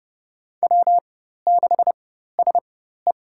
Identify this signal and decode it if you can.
Morse code, heard as a keyed tone, about 30 wpm, spaced as for 14 wpm, 700 Hz. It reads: W6SE
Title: USA Call Signs - 4 Characters - 4x Character Spacing - 30wpm